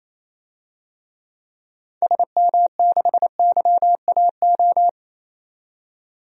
Morse code, heard as a keyed tone, about 28 wpm, 700 Hz. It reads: SM6YAO
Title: DXCC-100 Call Signs - 6 Characters - 28wpm